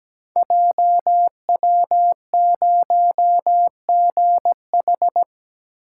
Morse code, heard as a keyed tone, about 17 wpm, 700 Hz. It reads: JW0GH
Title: Code Groups - 5 Letter:Numbers - 17wpm